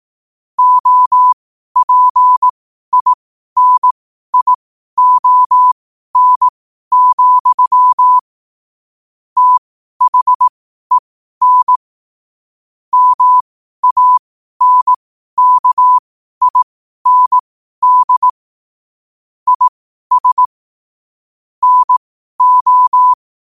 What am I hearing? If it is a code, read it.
Morse code, heard as a keyed tone, about 18 wpm, spaced as for 12 wpm, 1000 Hz. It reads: OPINION, THEN MANKIND IS NO
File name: qotdfiles / 12wpm